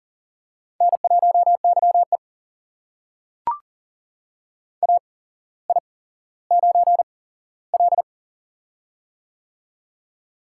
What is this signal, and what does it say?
Morse code, heard as a keyed tone, about 38 wpm, spaced as for 17 wpm, 700 Hz. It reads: N1YE AI9L
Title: USA Call Signs - 4 Characters - 8x Character Spacing - 40wpm